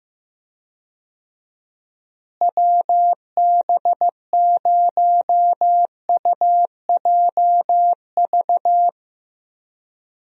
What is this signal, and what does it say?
Morse code, heard as a keyed tone, about 15 wpm, 700 Hz. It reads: WB0UJV
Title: USA Call Signs - 6 Characters - Repeated 2x - 15wpm